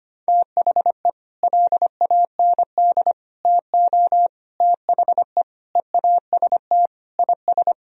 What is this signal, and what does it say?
Morse code, heard as a keyed tone, about 25 wpm, 700 Hz. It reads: THE LAND TO THE EAST IS
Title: Sentences from Top 1000 Words - Review Part 4 - 25wpm